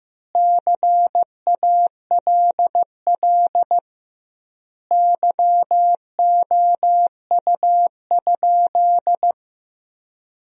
Morse code, heard as a keyed tone, about 15 wpm, 700 Hz. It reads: CALL YOU?
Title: 2x Word-Spacing - Sentences from Top 100 Words - 15wpm